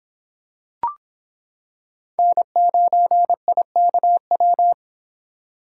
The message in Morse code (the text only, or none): N9IKW